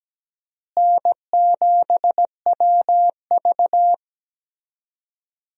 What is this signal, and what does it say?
Morse code, heard as a keyed tone, about 17 wpm, 700 Hz. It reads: N7WV
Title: USA Call Signs - 4 Characters - Repeated 2x - 17wpm